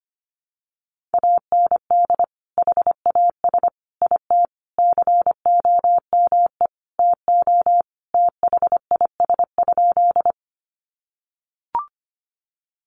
Morse code, heard as a keyed tone, about 25 wpm, 700 Hz. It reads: AND HAS IT COME TO THIS?
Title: Sentences from Top 100 Words 25wpm